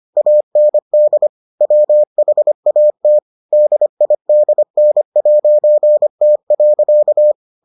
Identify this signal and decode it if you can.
Morse code, heard as a keyed tone, about 25 wpm, 600 Hz. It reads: AND WHAT DIDN'T.